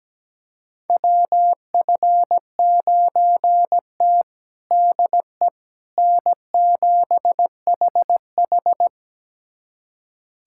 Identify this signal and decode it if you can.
Morse code, heard as a keyed tone, about 17 wpm, 700 Hz. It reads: WF9T DE N7HH
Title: USA Callsign DE Callsign - 4 Characters - 17wpm